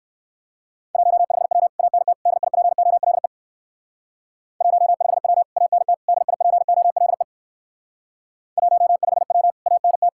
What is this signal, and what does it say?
Morse code, heard as a keyed tone, about 58 wpm, 700 Hz. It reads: 15W ANT DIPOLE 15W ANT DIPOLE 15W ANT